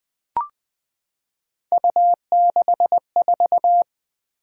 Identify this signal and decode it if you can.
Morse code, heard as a keyed tone, about 20 wpm, 700 Hz. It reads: U64
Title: Three Letter-Number Combinations - 20wpm